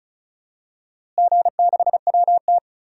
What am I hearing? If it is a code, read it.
Morse code, heard as a keyed tone, about 36 wpm, 700 Hz. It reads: G6WT
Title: DXCC-100 Call Signs - 4 Characters - Repeated 2x - 35wpm